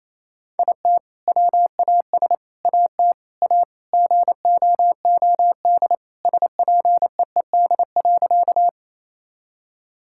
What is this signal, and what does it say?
Morse code, heard as a keyed tone, about 28 wpm, 700 Hz. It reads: IT WAS AT A GOOD SPEED.
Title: Sentences from Top 1000 Words - Review Part 5 - 28wpm